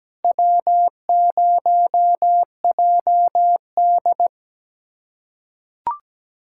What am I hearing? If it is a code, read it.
Morse code, heard as a keyed tone, about 17 wpm, 700 Hz. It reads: W0JD
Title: USA Callsign DE Callsign - 4 Characters - 17wpm